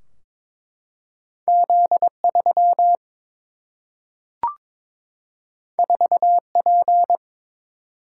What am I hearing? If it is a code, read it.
Morse code, heard as a keyed tone, about 22 wpm, 700 Hz. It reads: Z 3 4 P
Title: Single-Letter Single-Number 22wpm